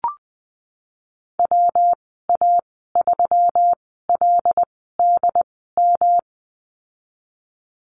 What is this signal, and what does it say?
Morse code, heard as a keyed tone, about 20 wpm, spaced as for 14 wpm, 700 Hz. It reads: WA3LDM
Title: 147 Lesson 24g - Letter y - Call Signs 20wpm